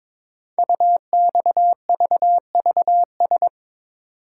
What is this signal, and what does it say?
Morse code, heard as a keyed tone, about 22 wpm, 700 Hz. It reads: UXVVS
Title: Code Groups - 5 Letters - 22wpm